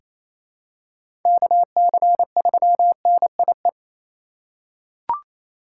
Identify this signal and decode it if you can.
Morse code, heard as a keyed tone, about 28 wpm, 700 Hz. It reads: KC3NIE
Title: K1USN SST QSO Practice - 28wpm